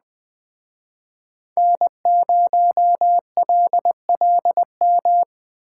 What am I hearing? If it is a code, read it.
Morse code, heard as a keyed tone, about 19 wpm, 700 Hz. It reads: N0LLM
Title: Speed Racing - USA 1x3 Call Signs 17wpm